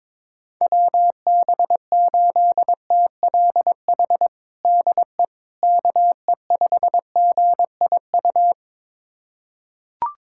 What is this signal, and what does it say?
Morse code, heard as a keyed tone, about 22 wpm, 700 Hz. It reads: WB8TLH DE KE5GIU
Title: USA Callsign DE Callsign - 6 Characters - 22wpm